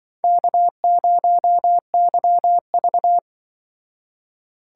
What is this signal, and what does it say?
Morse code, heard as a keyed tone, about 24 wpm, 700 Hz. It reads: K0YV
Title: Speed Racing - USA 1x2 Call Signs 20wpm